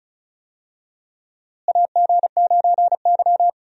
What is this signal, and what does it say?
Morse code, heard as a keyed tone, about 36 wpm, 700 Hz. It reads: AG9Y
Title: USA Call Signs - 4 Characters - Repeated 2x - 35wpm